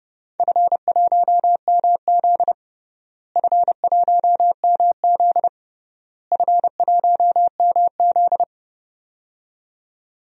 Morse code, heard as a keyed tone, about 30 wpm, 700 Hz. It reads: F1MZ F1MZ F1MZ
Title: DXCC-100 Call Signs - 4 Characters - Repeated 3x - 30wpm